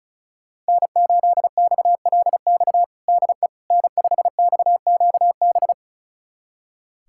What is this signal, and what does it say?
Morse code, heard as a keyed tone, about 36 wpm, 700 Hz. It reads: N8XLX DE N5XQB
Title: USA Callsign DE Callsign - 5 Characters - 35wpm